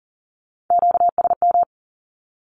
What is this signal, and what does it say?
Morse code, heard as a keyed tone, about 40 wpm, 700 Hz. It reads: QSM
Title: Q-Codes with Spoken Meaning - 40wpm